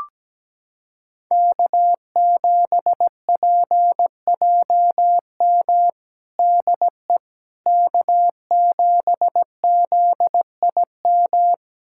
Morse code, heard as a keyed tone, about 17 wpm, 700 Hz. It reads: K7PJM DE K7ZIM
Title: USA Callsign DE Callsign - 5 Characters - 17wpm